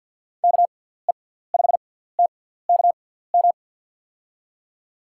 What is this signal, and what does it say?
Morse code, heard as a keyed tone, about 53 wpm, spaced as for 17 wpm, 700 Hz. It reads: KE5TKM
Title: USA Call Signs - 6 Characters - 6x Character Spacing - 50wpm